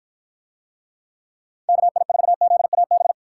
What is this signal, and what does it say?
Morse code, heard as a keyed tone, about 53 wpm, 700 Hz. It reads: KI4ZAB